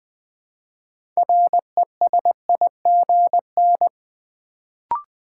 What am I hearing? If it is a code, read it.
Morse code, heard as a keyed tone, about 20 wpm, 700 Hz. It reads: RESIGN
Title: Build A Word - 3 Letters At A Time - 6-Letter Words - 20wpm